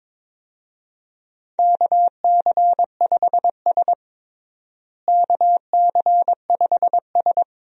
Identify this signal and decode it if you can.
Morse code, heard as a keyed tone, about 22 wpm, 700 Hz. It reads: KC5S KC5S
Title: USA Call Signs - 4 Characters - Repeated 2x - 22wpm